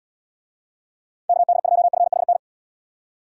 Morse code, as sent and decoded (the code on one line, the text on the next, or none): -.- -.. ..--- ..-. .... -.
KD2FHN